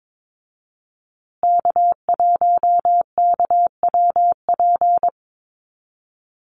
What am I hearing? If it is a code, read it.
Morse code, heard as a keyed tone, about 22 wpm, 700 Hz. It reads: K1KWP
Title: USA Call Signs Repeated 3x 22wpm